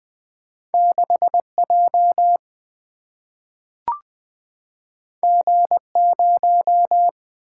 Morse code, heard as a keyed tone, about 20 wpm, 700 Hz. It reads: 6J G0
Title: Single-Letter Single-Number 20wpm